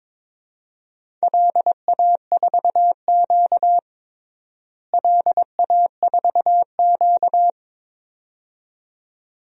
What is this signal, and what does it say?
Morse code, heard as a keyed tone, about 22 wpm, 700 Hz. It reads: LA4Q LA4Q
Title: DXCC-100 Call Signs - 4 Characters - Repeated 2x - 22wpm